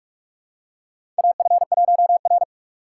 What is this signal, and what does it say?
Morse code, heard as a keyed tone, about 46 wpm, 700 Hz. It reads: AF1R
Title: USA Call Signs - 4 Characters - 6x Character Spacing - 45wpm